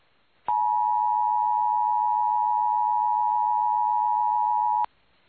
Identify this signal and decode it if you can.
Emergency Alert System (SAME) tones (the 853 and 960 Hz attention tone). It is heard from 0.4 to 4.9 s.